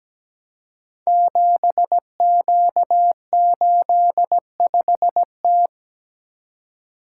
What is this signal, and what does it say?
Morse code, heard as a keyed tone, about 17 wpm, 700 Hz. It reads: 7Q85T
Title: Code Groups - 5 Letter:Numbers - 17wpm